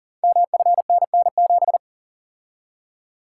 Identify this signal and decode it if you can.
Morse code, heard as a keyed tone, about 38 wpm, 690 Hz. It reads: MFNN7
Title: Code Groups - 5 Letter:Numbers - 40wpm